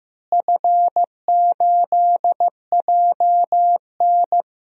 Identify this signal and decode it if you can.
Morse code, heard as a keyed tone, about 15 wpm, 700 Hz. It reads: F8JN
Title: DXCC-100 Call Signs - 4 Characters - Repeated 3x - 15wpm